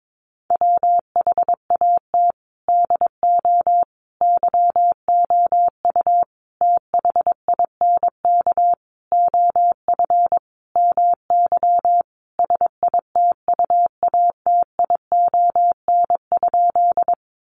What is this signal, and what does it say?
Morse code, heard as a keyed tone, about 22 wpm, 700 Hz. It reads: WHAT DO YOU THINK OF MY SITUATION?
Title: Sentences from Top 1200 Words 22wpm